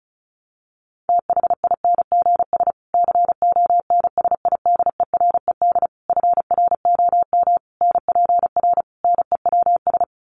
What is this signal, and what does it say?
Morse code, heard as a keyed tone, about 34 wpm, 700 Hz. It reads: THINGS CONSIDERED FROM NPR NEWS
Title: Sets of 5 Words Encore 35wpm